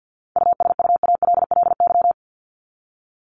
Morse code, heard as a keyed tone, about 50 wpm, 700 Hz. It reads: USUALLY
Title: Top 501-750 Words 50wpm